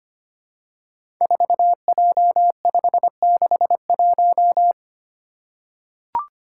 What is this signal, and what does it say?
Morse code, heard as a keyed tone, about 25 wpm, 700 Hz. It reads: 4J561
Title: Five Challenging Letter-Number Combinations 25wpm